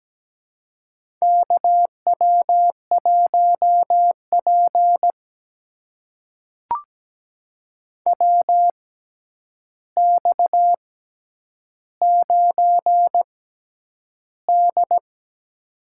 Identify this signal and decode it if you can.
Morse code, heard as a keyed tone, about 17 wpm, spaced as for 9 wpm, 700 Hz. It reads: KW1P WX9D
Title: USA Call Signs - 4 Characters - 6x Character Spacing - 17wpm